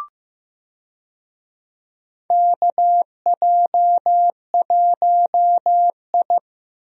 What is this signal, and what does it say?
Morse code, heard as a keyed tone, about 15 wpm, 700 Hz. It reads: KJ1I
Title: USA Call Signs - 4 Characters - Repeated 3x - 15wpm